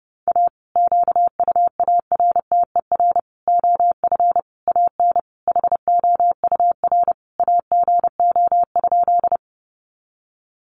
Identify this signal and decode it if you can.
Morse code, heard as a keyed tone, about 30 wpm, 700 Hz. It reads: A QUARTER OF AN HOUR AGO?
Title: Sentences from Top 1300 Words 30wpm